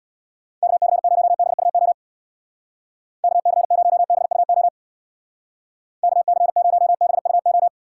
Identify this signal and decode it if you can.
Morse code, heard as a keyed tone, about 60 wpm, 700 Hz. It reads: KC0BUO KC0BUO KC0BUO
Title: Speed Racing - USA 2x3 Call Signs 50wpm